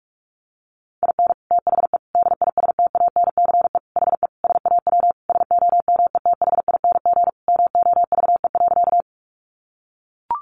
Morse code, heard as a keyed tone, about 44 wpm, 700 Hz. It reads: IN THE DISTANCE HE SAW SOMETHING MOVE.
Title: Sentences from Top 1200 Words 45wpm